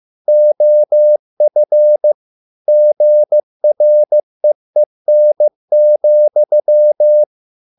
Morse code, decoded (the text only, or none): OF GREEN,